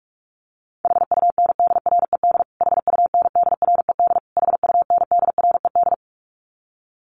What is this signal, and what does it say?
Morse code, heard as a keyed tone, about 44 wpm, 700 Hz. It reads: HUNDRED HUNDRED HUNDRED
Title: Top 500 Words Repeated 45wpm